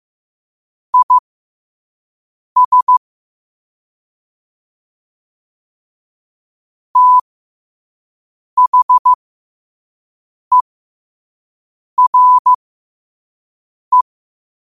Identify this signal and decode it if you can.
Morse code, heard as a keyed tone, about 15 wpm, spaced as for 5 wpm, 1000 Hz. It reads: IS THERE
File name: 5wpm